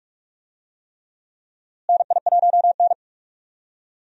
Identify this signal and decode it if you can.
Morse code, heard as a keyed tone, about 46 wpm, 700 Hz. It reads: NI1N
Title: USA Call Signs - 4 Characters - Repeated 3x - 45wpm